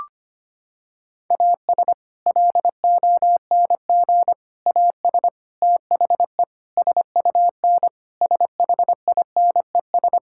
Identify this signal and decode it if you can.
Morse code, heard as a keyed tone, about 25 wpm, 700 Hz. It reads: AS LONG AS THE SUN SHINES
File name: practice / 25wpm / Sentences with 4-Word Phrases - 25wpm